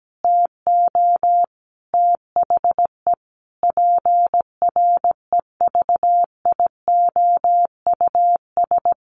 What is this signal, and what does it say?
Morse code, heard as a keyed tone, about 17 wpm, 700 Hz. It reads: TO THE PREVIOUS
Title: Sentences from Top 1500 Words 17wpm